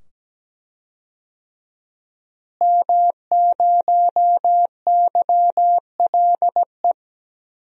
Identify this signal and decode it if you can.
Morse code, heard as a keyed tone, about 17 wpm, 700 Hz. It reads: M0YLE
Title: DXCC-100 Call Signs - 5 Characters - Repeated 2x - 17wpm